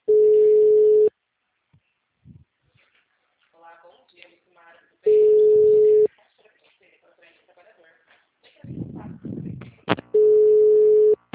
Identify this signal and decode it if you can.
Ringback tone (Europe, 425 Hz), with speech.
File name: out- -2012- - - .65